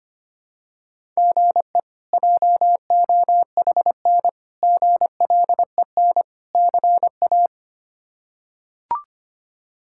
Morse code, heard as a keyed tone, about 25 wpm, 700 Hz. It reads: GE JOHN GLEN CA
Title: K1USN SST QSO Practice - 25wpm